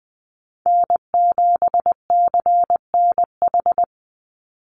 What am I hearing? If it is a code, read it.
Morse code, heard as a keyed tone, about 20 wpm, 700 Hz. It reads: N7CNH
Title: NA SOTA Chaser Callsigns 20wpm